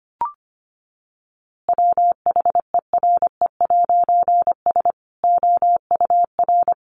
Morse code, heard as a keyed tone, about 25 wpm, 700 Hz. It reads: WHERE'S OUR
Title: A Sound Of Thunder 25wpm